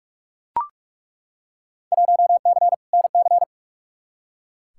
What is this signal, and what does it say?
Morse code, heard as a keyed tone, about 42 wpm, 700 Hz. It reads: 1C NC
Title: Field Day Exchanges - 45wpm